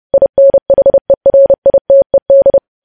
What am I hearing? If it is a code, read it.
Morse code, heard as a keyed tone, about 30 wpm, 550 Hz. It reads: INHERITED
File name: podcast / News headlines-1x-30x30